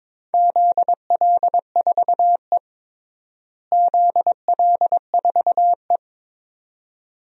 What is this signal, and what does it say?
Morse code, heard as a keyed tone, about 22 wpm, 700 Hz. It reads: ZL4E ZL4E
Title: DXCC-100 Call Signs - 4 Characters - Repeated 2x - 22wpm